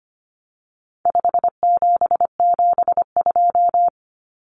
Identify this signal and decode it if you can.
Morse code, heard as a keyed tone, about 25 wpm, 700 Hz. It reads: 5772
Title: Common Cut Numbers - 4-Digit 25wpm